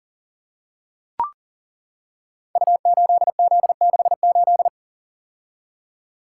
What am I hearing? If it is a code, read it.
Morse code, heard as a keyed tone, about 39 wpm, 700 Hz. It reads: U8Z68
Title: Code Groups - 5 Letter:Numbers - 40wpm